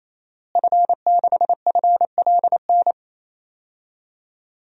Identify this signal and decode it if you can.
Morse code, heard as a keyed tone, about 27 wpm, 700 Hz. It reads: F6FLN
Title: DXCC-100 Call Signs - 5 Characters - Repeated 3x - 28wpm